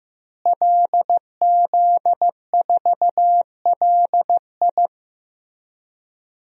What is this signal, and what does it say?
Morse code, heard as a keyed tone, about 15 wpm, 700 Hz. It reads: LZ4LI
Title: DXCC-100 Call Signs - 5 Characters - Repeated 3x - 15wpm